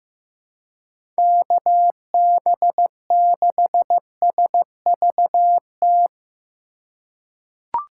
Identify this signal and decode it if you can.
Morse code, heard as a keyed tone, about 15 wpm, 700 Hz. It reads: KB6SVT
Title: USA Call Signs - 6 Characters - 4x Character Spacing - 15wpm